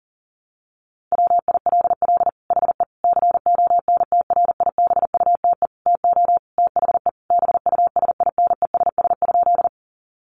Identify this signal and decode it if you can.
Morse code, heard as a keyed tone, about 37 wpm, 700 Hz. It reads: WILL HE CONTRIBUTE TO THE BUSINESS?
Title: Sentences from Top 1400 Words 40wpm